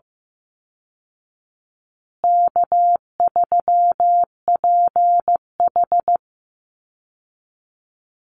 Morse code, heard as a keyed tone, about 15 wpm, 700 Hz. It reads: K3PH
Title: Easy Call Signs Repeated 3x - 15wpm